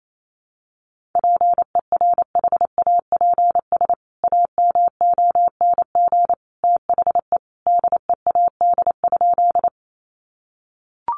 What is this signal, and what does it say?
Morse code, heard as a keyed tone, about 28 wpm, 700 Hz. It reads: PERHAPS AMONG THE DEAD?
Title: Sentences from Top 1500 Words 28wpm